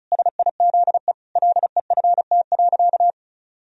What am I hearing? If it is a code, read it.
Morse code, heard as a keyed tone, about 36 wpm, 700 Hz. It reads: SIZE LEFT.
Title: Sentences from Top 1000 Words - Review Part 2 - 35wpm